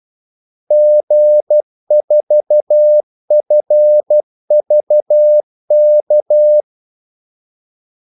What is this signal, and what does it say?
Morse code, heard as a keyed tone, about 12 wpm, 600 Hz. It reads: G4FVK
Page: 2 ekstra ordmellemrum mellem kaldesignalerne